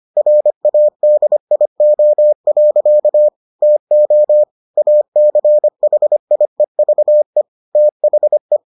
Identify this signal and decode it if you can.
Morse code, heard as a keyed tone, about 25 wpm, 600 Hz. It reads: RADIO. TO ACHIEVE THE